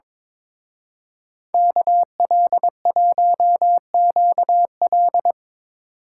Morse code, heard as a keyed tone, about 25 wpm, 700 Hz. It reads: KL1QL
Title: Speed Racing - USA 2x2 Call Signs 22wpm